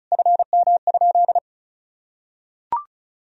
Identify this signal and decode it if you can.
Morse code, heard as a keyed tone, about 34 wpm, 700 Hz. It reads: FM?
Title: Sentences from Top 200 Words and Abbreviations 35wpm